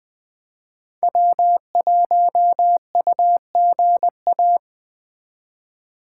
Morse code, heard as a keyed tone, about 20 wpm, 700 Hz. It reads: W1UGA